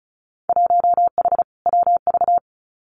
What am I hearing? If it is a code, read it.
Morse code, heard as a keyed tone, about 34 wpm, 700 Hz. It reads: 1H WV
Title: Winter Field Day Exchanges 35wpm